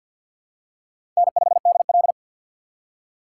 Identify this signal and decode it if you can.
Morse code, heard as a keyed tone, about 48 wpm, 700 Hz. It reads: N5DL
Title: USA Call Signs - 4 Characters - 6x Character Spacing - 50wpm